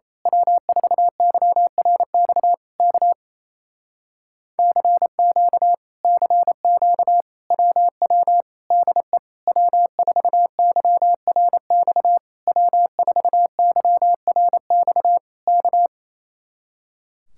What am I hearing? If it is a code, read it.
Morse code, heard as a keyed tone, about 27 wpm, 700 Hz. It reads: W4YRX K CQ CQ WW DE W4YRX W4YRX K
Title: Speed-Racing Common QSO Phrases Encore - 28wpm